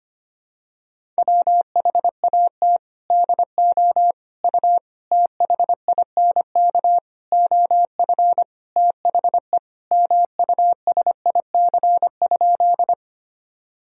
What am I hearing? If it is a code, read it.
Morse code, heard as a keyed tone, about 25 wpm, 700 Hz. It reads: WHAT DO U THINK OF THE MUSIC?
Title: Sentences from Top 300 Words and Abbreviations 25wpm